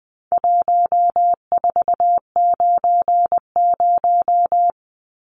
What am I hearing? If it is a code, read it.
Morse code, heard as a keyed tone, about 20 wpm, 700 Hz. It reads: 1490
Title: Numbers - 4-Digit 20wpm